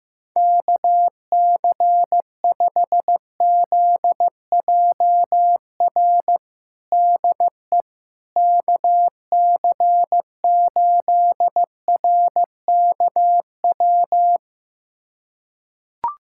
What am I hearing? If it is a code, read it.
Morse code, heard as a keyed tone, about 16 wpm, 700 Hz. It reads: KC5ZJR DE KC8RKW
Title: USA Callsign DE Callsign - 6 Characters - 15wpm